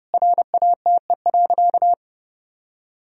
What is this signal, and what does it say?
Morse code, heard as a keyed tone, about 30 wpm, 700 Hz. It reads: RATE.
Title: Sentences from Top 1000 Words - Review Part 4 - 30wpm